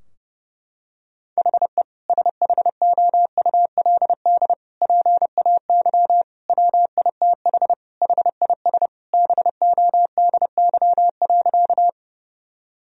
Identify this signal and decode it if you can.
Morse code, heard as a keyed tone, about 30 wpm, 700 Hz. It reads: HE SHOULD PAY WITH HIS BODY.
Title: Sentences from Top 1000 Words - Review Part 2 - 30wpm